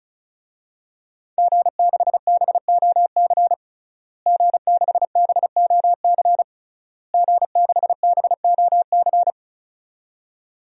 Morse code, heard as a keyed tone, about 34 wpm, 700 Hz. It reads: G6BOC G6BOC G6BOC
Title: DXCC-100 Call Signs - 5 Characters - Repeated 3x - 35wpm